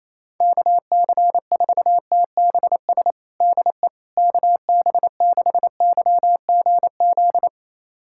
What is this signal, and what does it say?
Morse code, heard as a keyed tone, about 28 wpm, 700 Hz. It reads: KC4TBS DE KB6YGZ
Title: USA Callsign DE Callsign - 6 Characters - 28wpm